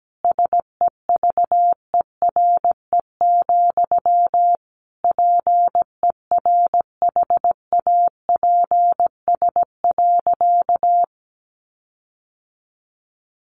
Morse code, heard as a keyed tone, about 17 wpm, 700 Hz. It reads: SEVERE, PERHAPS.
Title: Sentences from Top 1400 Words 17wpm